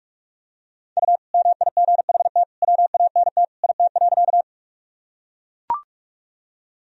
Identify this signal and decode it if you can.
Morse code, heard as a keyed tone, about 44 wpm, 700 Hz. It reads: U MIGHT WANT IT.
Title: Sentences from Top 300 Words and Abbreviations 45wpm